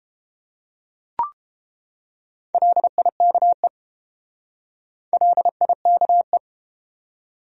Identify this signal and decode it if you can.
Morse code, heard as a keyed tone, about 31 wpm, 700 Hz. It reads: LIKE LIKE